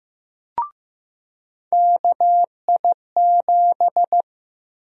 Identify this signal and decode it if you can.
Morse code, heard as a keyed tone, about 15 wpm, 700 Hz. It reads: KI7
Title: USA Call Sign Prefixes - 15wpm